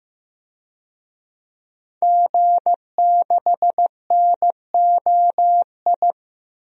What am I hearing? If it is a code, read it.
Morse code, heard as a keyed tone, about 15 wpm, 700 Hz. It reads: G6NOI
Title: DXCC-100 Call Signs - 5 Characters - Repeated 3x - 15wpm